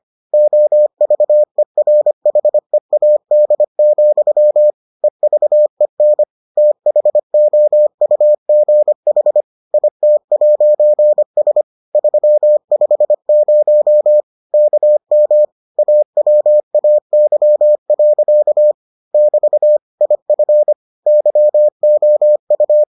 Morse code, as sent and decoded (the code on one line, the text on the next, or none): --- ...- . .-. .... .- -.. --..-- / . ...- . -. / - .... --- ..- --. .... / .. - .----. ... / ...-- ..... ----- / -.- -- / .- .-- .- -.-- .-.-.- / -...- / .. ..-. / -.-- --- ..-
OVERHEAD, EVEN THOUGH IT'S 350 KM AWAY. <BT> IF YOU